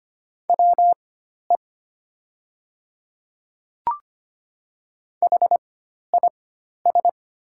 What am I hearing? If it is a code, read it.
Morse code, heard as a keyed tone, about 25 wpm, spaced as for 12 wpm, 700 Hz. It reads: WE HIS
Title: 4x Character Spacing - Top 200 Words - 25wpm